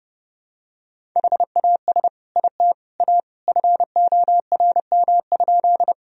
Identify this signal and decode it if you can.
Morse code, heard as a keyed tone, about 30 wpm, 700 Hz. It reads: HAS IT A FORM?
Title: Sentences from Top 1000 Words - Review Part 1 - 30wpm